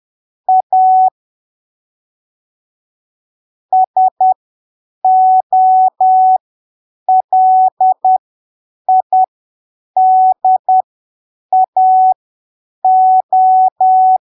Morse code, heard as a keyed tone, about 10 wpm, spaced as for 7 wpm, 750 Hz. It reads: A SOLIDAO
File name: arquivos / texto / texto 23 10ppm